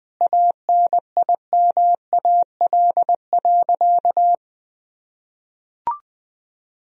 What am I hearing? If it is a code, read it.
Morse code, heard as a keyed tone, about 20 wpm, 700 Hz. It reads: ANIMAL.
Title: Sentences from Top 1000 Words - Review Part 1 - 20wpm